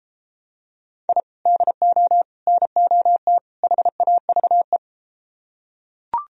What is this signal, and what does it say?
Morse code, heard as a keyed tone, about 34 wpm, 700 Hz. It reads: I DO NOT HAVE